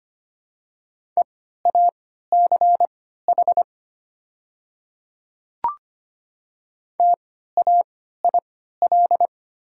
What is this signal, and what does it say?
Morse code, heard as a keyed tone, about 25 wpm, spaced as for 14 wpm, 700 Hz. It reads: EACH TAIL
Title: 3x Character Spacing - Top 500 Words - 25wpm